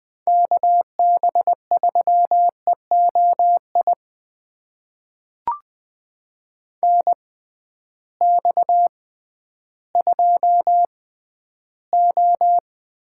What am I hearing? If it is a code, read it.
Morse code, heard as a keyed tone, about 20 wpm, spaced as for 12 wpm, 700 Hz. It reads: KB3EOI NX2O
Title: USA Call Signs - 6x Character Spacing - 20wpm